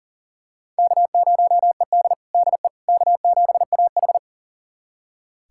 Morse code, heard as a keyed tone, about 40 wpm, 700 Hz. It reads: K0ED DE K7AH
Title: USA Callsign DE Callsign - 4 Characters - 40wpm